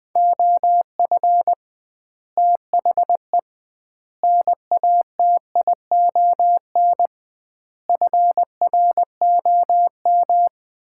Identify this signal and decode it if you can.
Morse code, heard as a keyed tone, about 20 wpm, 700 Hz. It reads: OF THE NATION FROM
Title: 2x Word-Spacing - Sets of 5 Words - 20wpm